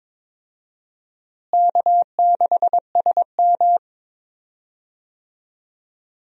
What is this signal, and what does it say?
Morse code, heard as a keyed tone, about 22 wpm, 700 Hz. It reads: K6SM
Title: USA Call Signs - 4 Characters - Repeated 2x - 22wpm